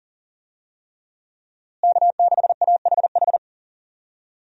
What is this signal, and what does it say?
Morse code, heard as a keyed tone, about 40 wpm, 700 Hz. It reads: K6AHH
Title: USA Call Signs - 5 Characters - Repeated 3x - 40wpm